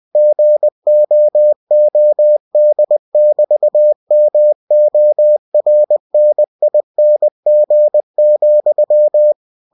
Morse code, heard as a keyed tone, about 20 wpm, 600 Hz. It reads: GOOD<BT>MORNING,
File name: cw-books / winnie-the-pooh / winnie-20wpm-page-0076